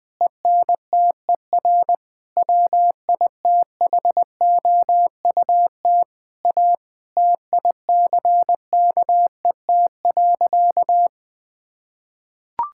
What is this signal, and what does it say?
Morse code, heard as a keyed tone, about 21 wpm, 700 Hz. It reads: ENTER WITHOUT A TICKET.
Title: Sentences with 6-Word Phrases - Spoken Beginning - 20wpm